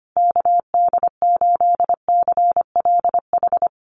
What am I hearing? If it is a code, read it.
Morse code, heard as a keyed tone, about 25 wpm, 700 Hz. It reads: KD8CLH
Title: USA Call Signs Repeated 2x 25wpm